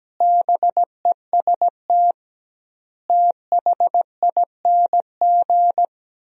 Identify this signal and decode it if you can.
Morse code, heard as a keyed tone, about 17 wpm, 700 Hz. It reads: BEST THING
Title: 2x Word-Spacing - Sentences from Top 400 Words - 17wpm